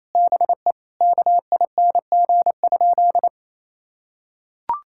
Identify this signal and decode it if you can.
Morse code, heard as a keyed tone, about 28 wpm, 700 Hz. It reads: BE KING?
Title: Sentences from Top 1600 Words 28wpm